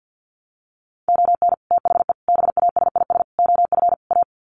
Morse code, heard as a keyed tone, about 50 wpm, 700 Hz. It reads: ON THE BASIS OF A
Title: Sets of 5 Words Encore 50wpm